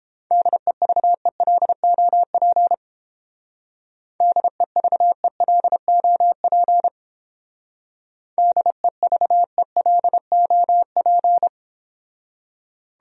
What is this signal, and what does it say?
Morse code, heard as a keyed tone, about 30 wpm, 700 Hz. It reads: DEVELOP DEVELOP DEVELOP